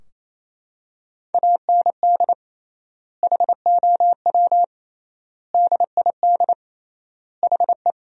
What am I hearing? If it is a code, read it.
Morse code, heard as a keyed tone, about 28 wpm, 700 Hz. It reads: AND HOW DID HE
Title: 3x Word-Spacing - Sentences from Top 400 Words - 28wpm